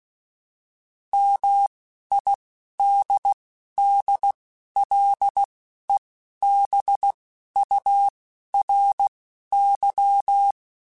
Morse code, heard as a keyed tone, about 16 wpm, spaced as for 11 wpm, 800 Hz. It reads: MIDDLEBURY
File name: fromSMs / FromSMs mp3s / Cities@10